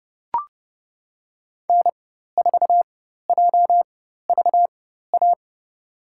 Morse code, heard as a keyed tone, about 30 wpm, spaced as for 14 wpm, 700 Hz. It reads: N4JVA
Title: USA Call Signs - 5 Characters - 4x Character Spacing - 30wpm